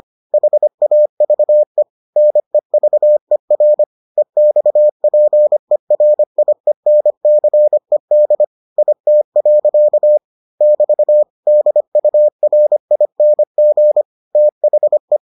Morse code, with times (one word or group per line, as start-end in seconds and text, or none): 0.3-1.8 HAVE
2.2-3.8 NEVER
4.2-8.4 EXPERIENCED
8.8-10.2 IT.
10.6-11.2 <BT>
11.5-14.0 DURING
14.3-15.2 THE